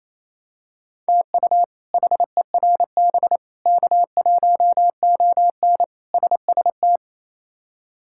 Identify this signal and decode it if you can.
Morse code, heard as a keyed tone, about 28 wpm, 700 Hz. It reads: TU HERB K1ON SST
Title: K1USN SST QSO Practice - 28wpm